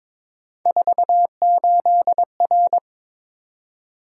Morse code, heard as a keyed tone, about 22 wpm, 700 Hz. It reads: 48R
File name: practice / 22wpm / Three Letter-Number Combinations - 22wpm